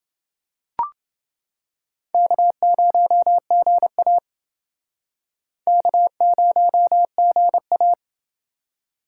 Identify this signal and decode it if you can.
Morse code, heard as a keyed tone, about 28 wpm, 700 Hz. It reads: K0GA K0GA